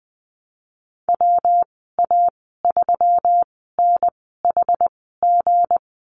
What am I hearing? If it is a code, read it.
Morse code, heard as a keyed tone, about 20 wpm, spaced as for 15 wpm, 700 Hz. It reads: WA3NHG
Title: 167 Lesson 27g - Number 7 - Call Signs 20wpm